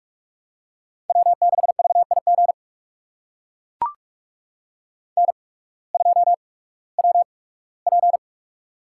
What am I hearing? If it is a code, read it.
Morse code, heard as a keyed tone, about 44 wpm, spaced as for 21 wpm, 700 Hz. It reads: W6VIG N2WP
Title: USA Call Signs - 8x Character Spacing - 45wpm